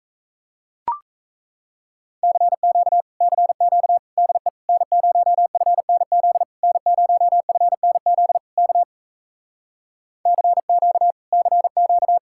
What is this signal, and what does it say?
Morse code, heard as a keyed tone, about 38 wpm, 700 Hz. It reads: CQ CQ DE N0FNZ N0FNZ K CQ CQ